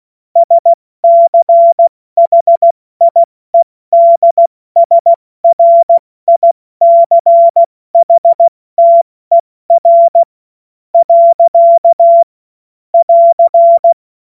Morse code, heard as a keyed tone, about 16 wpm, spaced as for 14 wpm, 680 Hz. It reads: SCHIEDSRICHTER. <AR>
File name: Morsekurs / Witze / Witz 02. Februar 2021, 70 F80